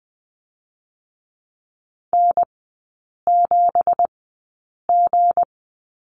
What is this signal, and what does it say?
Morse code, heard as a keyed tone, about 20 wpm, spaced as for 8 wpm, 700 Hz. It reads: N7G
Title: 178 Lesson 29e - Slash - Three Characters 20wpm